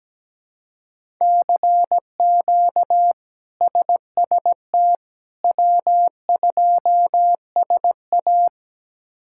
Morse code, heard as a keyed tone, about 17 wpm, 700 Hz. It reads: CQ SST W2SA
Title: K1USN SST QSO Practice - 17wpm